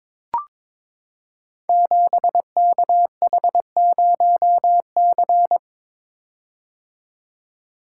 Morse code, heard as a keyed tone, about 23 wpm, 700 Hz. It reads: 7KH0C
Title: Code Groups - 5 Letter:Numbers - 22wpm